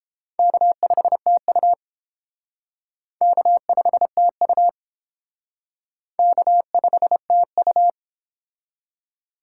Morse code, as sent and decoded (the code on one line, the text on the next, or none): -.- ..... - ..- / -.- ..... - ..- / -.- ..... - ..-
K5TU K5TU K5TU